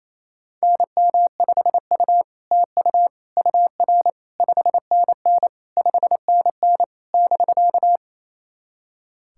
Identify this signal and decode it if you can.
Morse code, heard as a keyed tone, about 28 wpm, 700 Hz. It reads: NM5U TU UR 5NN 5NN <BK>
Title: SOTA QSO Practice 28wpm